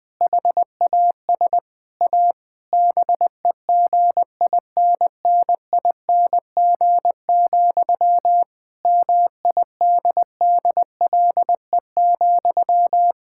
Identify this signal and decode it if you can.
Morse code, heard as a keyed tone, about 21 wpm, 700 Hz. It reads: HAS A BEGINNING, MIDDLE,
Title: Sentences from Top 1000 Words - Review Part 6 - 20wpm